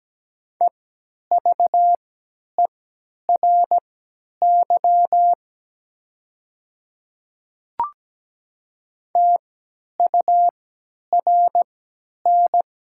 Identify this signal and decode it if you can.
Morse code, heard as a keyed tone, about 17 wpm, spaced as for 10 wpm, 700 Hz. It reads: EVERY TURN
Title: 3x Character Spacing - Top 500 Words - 17wpm